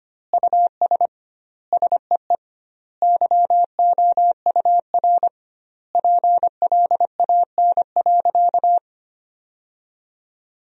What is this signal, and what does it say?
Morse code, heard as a keyed tone, about 25 wpm, 700 Hz. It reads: US SEE YOUR PLAN.
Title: 2x Word-Spacing - Sentences from Top 500 Words - 25wpm